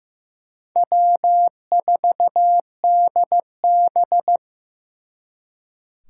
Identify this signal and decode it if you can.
Morse code, heard as a keyed tone, about 18 wpm, 700 Hz. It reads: W4DB
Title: Speed Racing - USA 1x2 Call Signs 15wpm